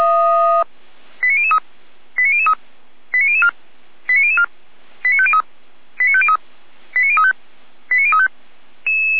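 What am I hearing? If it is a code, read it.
Five-tone selective call (ZVEI), heard as quick runs of five tones, tones of about 80 ms. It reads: 89003 89003 89004 89004 89503 89503 89035 89035
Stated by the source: Five-tone ZVEI selective call 89099 received on SDR, from the server of feuerwehr-orlamuende (fire brigade Orlamünde).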